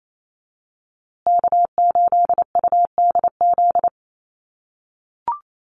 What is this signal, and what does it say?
Morse code, heard as a keyed tone, about 28 wpm, 700 Hz. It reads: K8UDZ
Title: USA Call Signs Repeated 3x 28wpm